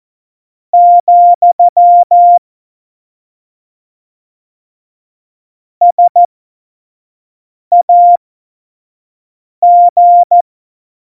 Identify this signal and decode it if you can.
Morse code, heard as a keyed tone, about 14 wpm, spaced as for 5 wpm, 700 Hz. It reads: , SAG
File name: Morsekurs / Witze / Witz 22. Mai 2018, 25 F70